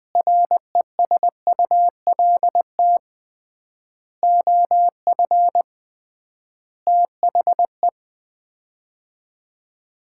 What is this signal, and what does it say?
Morse code, heard as a keyed tone, about 20 wpm, 700 Hz. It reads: RESULT OF THE
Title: 3x Word-Spacing - Sets of 5 Words - 20wpm